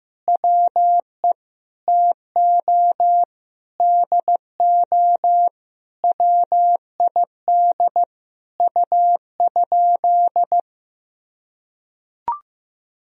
Morse code, heard as a keyed tone, about 15 wpm, 700 Hz. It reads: WE TO DO WID U?
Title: Sentences from Top 100 Words and Abbreviations 15wpm